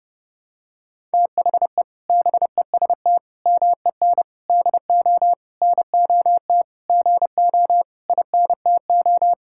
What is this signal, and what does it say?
Morse code, heard as a keyed tone, about 30 wpm, 700 Hz. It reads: THE BEST MEN DO NOT GO INTO
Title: Sentences from Top 1000 Words - Review Part 6 - 30wpm